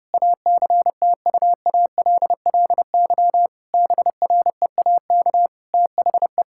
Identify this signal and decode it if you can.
Morse code, heard as a keyed tone, about 30 wpm, 700 Hz. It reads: ACTUALLY BREAK THE
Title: Sentences from Top 1000 Words - Review Part 6 - 30wpm